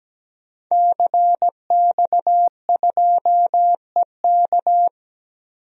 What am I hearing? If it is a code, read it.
Morse code, heard as a keyed tone, about 17 wpm, 700 Hz. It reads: CX2EK
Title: DXCC-100 Call Signs - 5 Characters - 17wpm